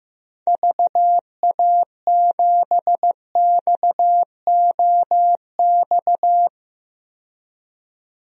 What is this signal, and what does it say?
Morse code, heard as a keyed tone, about 15 wpm, 700 Hz. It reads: VA7XOX
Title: DXCC-100 Call Signs - 6 Characters - 15wpm